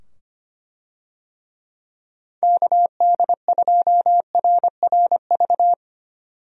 Morse code, heard as a keyed tone, about 25 wpm, 700 Hz. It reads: KD2RRV
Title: USA Call Signs - 6 Characters - 25wpm